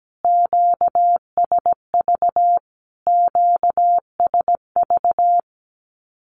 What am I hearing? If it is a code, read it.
Morse code, heard as a keyed tone, about 17 wpm, 700 Hz. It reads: QSV QSV
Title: Q-Codes Repeated 3x 17wpm